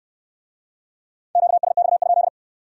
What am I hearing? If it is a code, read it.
Morse code, heard as a keyed tone, about 61 wpm, 700 Hz. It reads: QSX?